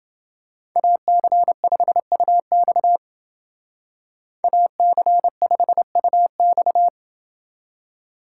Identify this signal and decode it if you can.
Morse code, heard as a keyed tone, about 29 wpm, 700 Hz. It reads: AC5UX AC5UX